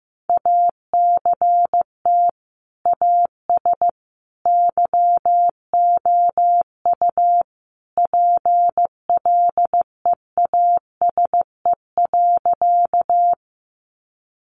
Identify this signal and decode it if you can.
Morse code, heard as a keyed tone, about 15 wpm, 700 Hz. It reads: ACT AS YOU PLEASE.
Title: Sentences from Top 1500 Words 15wpm